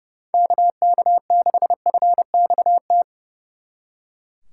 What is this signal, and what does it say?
Morse code, heard as a keyed tone, about 36 wpm, 700 Hz. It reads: KK6FXT
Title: Speed Racing - USA Call Signs 30wpm